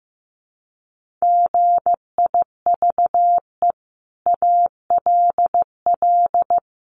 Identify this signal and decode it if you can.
Morse code, heard as a keyed tone, about 15 wpm, 700 Hz. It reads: GIVE ALL
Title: Sentences from Top 600 Words 15wpm